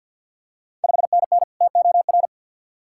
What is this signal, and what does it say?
Morse code, heard as a keyed tone, about 53 wpm, 690 Hz. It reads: 5NN TOR
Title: Ontario QSO Party Exchange 50wpm